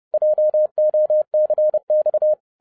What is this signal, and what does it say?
Morse code, heard as a keyed tone, about 30 wpm, 610 Hz. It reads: JOCX